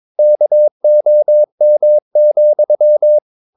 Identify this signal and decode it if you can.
Morse code, heard as a keyed tone, about 22 wpm, 600 Hz. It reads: KOM,
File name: Ferie QRS 22wpm